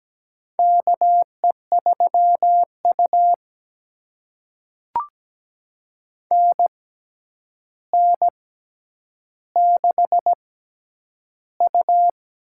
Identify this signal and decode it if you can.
Morse code, heard as a keyed tone, about 17 wpm, spaced as for 9 wpm, 700 Hz. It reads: KE3U NN6U
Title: USA Call Signs - 4 Characters - 6x Character Spacing - 17wpm